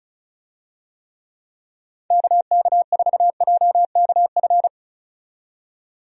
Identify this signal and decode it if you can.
Morse code, heard as a keyed tone, about 36 wpm, 700 Hz. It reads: KK4JKF
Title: USA Call Signs - 6 Characters - 35wpm